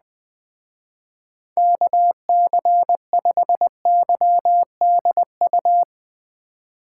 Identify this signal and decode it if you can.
Morse code, heard as a keyed tone, about 20 wpm, 700 Hz. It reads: KC5YDU
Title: Speed Racing - USA 2x3 Call Signs 17wpm